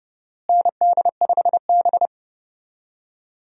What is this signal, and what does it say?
Morse code, heard as a keyed tone, about 30 wpm, 700 Hz. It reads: ND5B
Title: USA Call Signs - 4 Characters - Repeated 3x - 30wpm